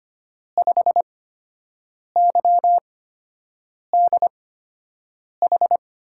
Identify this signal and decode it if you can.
Morse code, heard as a keyed tone, about 25 wpm, spaced as for 7 wpm, 700 Hz. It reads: 5YDH
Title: USA Call Signs - 6 Characters - 8x Character Spacing - 25wpm